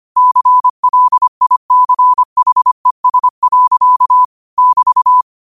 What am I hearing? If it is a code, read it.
Morse code, heard as a keyed tone, about 25 wpm, 1000 Hz. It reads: CLICHES. <BT>
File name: qotdfiles / 25wpm